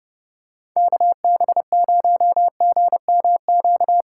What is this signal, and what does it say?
Morse code, heard as a keyed tone, about 30 wpm, 700 Hz. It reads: KB0GMQ
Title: USA Call Signs - 2x Character Spacing - 30wpm